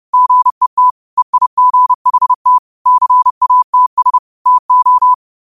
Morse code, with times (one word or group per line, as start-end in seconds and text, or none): 0.1-0.9 GET
1.2-2.6 EIGHT
2.9-4.2 CATS
4.5-5.1 TO